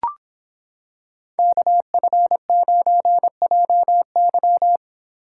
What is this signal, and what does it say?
Morse code, heard as a keyed tone, about 26 wpm, 700 Hz. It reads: KF9JY